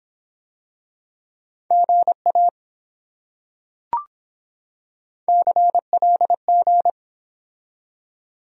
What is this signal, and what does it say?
Morse code, heard as a keyed tone, about 24 wpm, 700 Hz. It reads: GA CLG